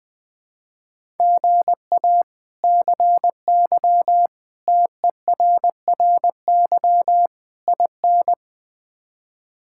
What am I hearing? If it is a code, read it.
Morse code, heard as a keyed tone, about 20 wpm, 700 Hz. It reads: GA CY TERRY IN
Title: K1USN SST QSO Practice - 20wpm